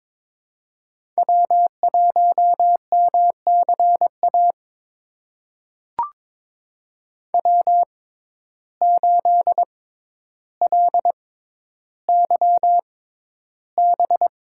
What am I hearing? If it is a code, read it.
Morse code, heard as a keyed tone, about 22 wpm, spaced as for 11 wpm, 700 Hz. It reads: W1MCA W8LYB
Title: USA Call Signs - 6x Character Spacing - 22wpm